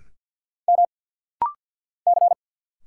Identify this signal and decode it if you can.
Morse code, heard as a keyed tone, about 50 wpm, 700 Hz. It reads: M C
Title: Single Letters - Rapid-Fire 50wpm